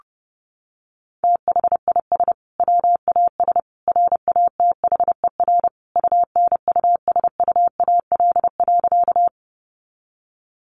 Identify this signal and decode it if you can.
Morse code, heard as a keyed tone, about 30 wpm, 700 Hz. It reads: THIS WAS RATHER UNUSUAL.
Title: Sentences from Top 1400 Words 30wpm